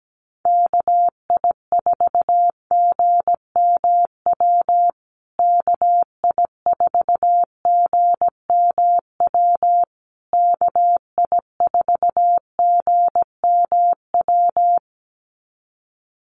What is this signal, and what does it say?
Morse code, heard as a keyed tone, about 17 wpm, 700 Hz. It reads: KI4GMW KI4GMW KI4GMW
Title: USA Call Signs Repeated 3x 17wpm